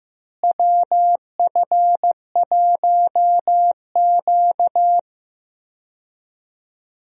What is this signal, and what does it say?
Morse code, heard as a keyed tone, about 15 wpm, 700 Hz. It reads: WF1Q
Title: USA Call Signs - 4 Characters - Repeated 3x - 15wpm